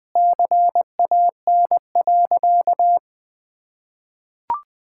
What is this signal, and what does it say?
Morse code, heard as a keyed tone, about 20 wpm, 700 Hz. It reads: CAN.
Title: Sentences with 2-Word Phrases - Spoken Beginning - 20wpm